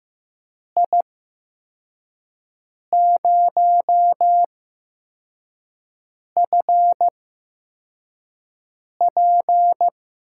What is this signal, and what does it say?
Morse code, heard as a keyed tone, about 15 wpm, spaced as for 4 wpm, 700 Hz. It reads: I0FP
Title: USA Call Signs - 5 Characters - 8x Character Spacing - 15wpm